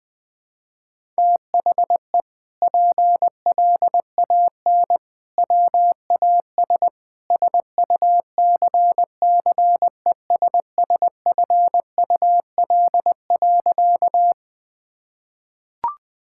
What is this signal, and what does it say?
Morse code, heard as a keyed tone, about 20 wpm, 700 Hz. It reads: THE PLAN WAS SUCCESSFUL.
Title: Sentences from Top 1000 Words - Review Part 6 - 20wpm